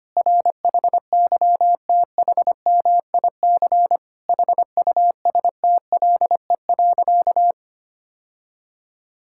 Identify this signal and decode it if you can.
Morse code, heard as a keyed tone, about 25 wpm, 700 Hz. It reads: RHYTHMIC HUSTLE.
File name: practice / 25wpm / Difficult Sentences with Hints - 25wpm